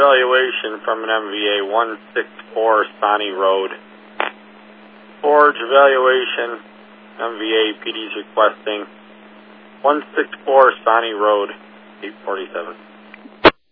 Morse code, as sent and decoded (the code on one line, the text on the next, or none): none